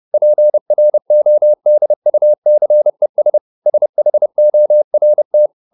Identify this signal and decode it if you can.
Morse code, heard as a keyed tone, about 30 wpm, 600 Hz. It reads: PRODUCES SHORT